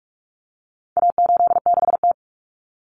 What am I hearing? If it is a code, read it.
Morse code, heard as a keyed tone, about 44 wpm, 700 Hz. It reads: A86T
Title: Common Cut Numbers - 4-Digit 45wpm